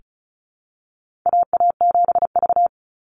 Morse code, heard as a keyed tone, about 34 wpm, 700 Hz. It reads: AA7V
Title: Call Signs Repeated 2x - 35wpm